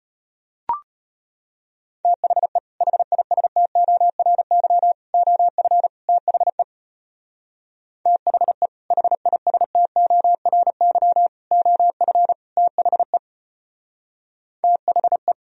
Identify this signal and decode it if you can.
Morse code, heard as a keyed tone, about 33 wpm, 700 Hz. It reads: THE HISTORY OF THE THE HISTORY OF THE THE